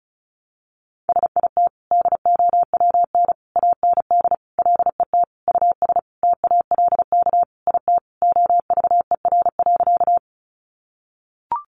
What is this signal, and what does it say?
Morse code, heard as a keyed tone, about 34 wpm, 700 Hz. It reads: SIT DOWN AND LET US TALK IT OVER.
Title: Sentences from Top 500 Words 35wpm